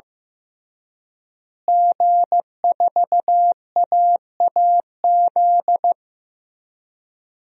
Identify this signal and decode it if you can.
Morse code, heard as a keyed tone, about 15 wpm, 700 Hz. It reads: G4AAZ
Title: DXCC-100 Call Signs - 5 Characters - Repeated 2x - 15wpm